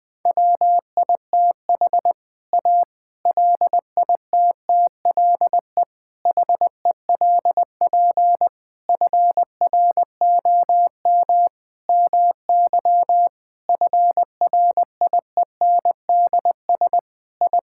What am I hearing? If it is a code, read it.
Morse code, heard as a keyed tone, about 20 wpm, 700 Hz. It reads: WITH A LITTLE HELP FROM MY FRIENDS I
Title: Sentences with 7-Word Phrases - Spoken Beginning - 20wpm